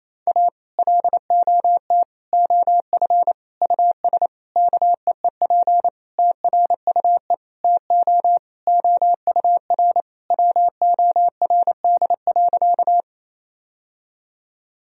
Morse code, heard as a keyed tone, about 28 wpm, 700 Hz. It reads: A LOT OF US KEEP TRUE TO OUR WORD.
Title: Sentences from Top 1000 Words - Review Part 3 - 28wpm